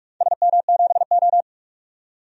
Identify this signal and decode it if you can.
Morse code, heard as a keyed tone, about 48 wpm, 700 Hz. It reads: SM7O
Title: DXCC-100 Call Signs - 4 Characters - 45wpm